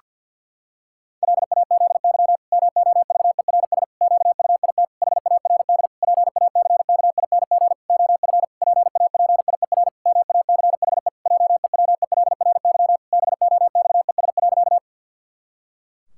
Morse code, heard as a keyed tone, about 50 wpm, 700 Hz. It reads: LAZY MOVERS QUIT HARD PACKING OF PAPIER MACHE JEWELRY BOXES.
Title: Pangram Challenge v2 - 50wpm